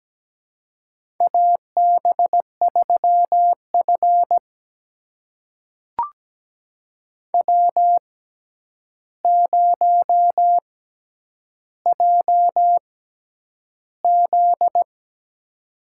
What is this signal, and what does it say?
Morse code, heard as a keyed tone, about 17 wpm, spaced as for 9 wpm, 700 Hz. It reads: AB3F W0JZ
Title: USA Call Signs - 4 Characters - 6x Character Spacing - 17wpm